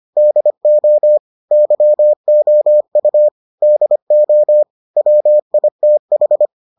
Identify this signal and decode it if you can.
Morse code, heard as a keyed tone, about 25 wpm, 600 Hz. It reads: DO YOU DO WITH